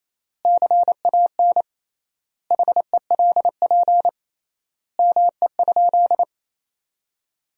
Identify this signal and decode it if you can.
Morse code, heard as a keyed tone, about 27 wpm, 700 Hz. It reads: CAN HELP ME?
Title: 3x Word-Spacing - Sentences from Top 200 Words - 28wpm